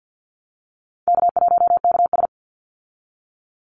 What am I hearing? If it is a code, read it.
Morse code, heard as a keyed tone, about 50 wpm, 700 Hz. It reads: K1KS
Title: Easy Call Signs Repeated 2x - 50wpm